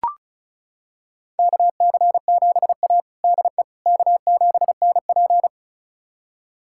Morse code, heard as a keyed tone, about 36 wpm, 700 Hz. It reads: KC7A DE K7NP
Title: USA Callsign DE Callsign - 4 Characters - 35wpm